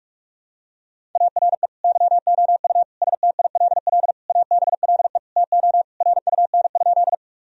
Morse code, heard as a keyed tone, about 41 wpm, 700 Hz. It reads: ARE YOU STILL ABLE TO RUN?
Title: Sentences from Top 1000 Words - Review Part 2 - 45wpm